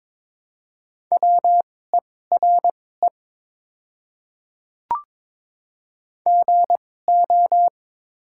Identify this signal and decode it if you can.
Morse code, heard as a keyed tone, about 22 wpm, spaced as for 16 wpm, 700 Hz. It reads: WERE GO
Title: 2x Character Spacing - Top 100 Words - 22wpm